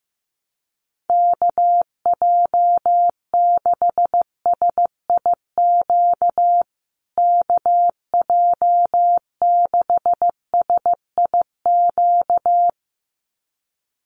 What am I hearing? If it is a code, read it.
Morse code, heard as a keyed tone, about 15 wpm, 700 Hz. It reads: KJ6SIQ KJ6SIQ
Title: USA Call Signs Repeated 2x 15wpm